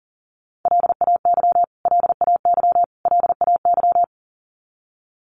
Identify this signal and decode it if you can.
Morse code, heard as a keyed tone, about 40 wpm, 700 Hz. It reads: LAY LAY LAY
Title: Top 500 Words Repeated 40wpm